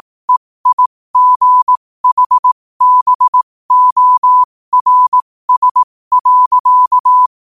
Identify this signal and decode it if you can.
Morse code, heard as a keyed tone, about 18 wpm, spaced as for 15 wpm, 1000 Hz. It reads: EIGHBORS.
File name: qotdfiles / 15wpm